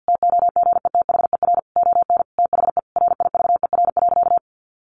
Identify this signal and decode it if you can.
Morse code, heard as a keyed tone, about 48 wpm, 700 Hz. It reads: TOGETHER ON THE RIVER.
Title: Sentences from Top 1300 Words 50wpm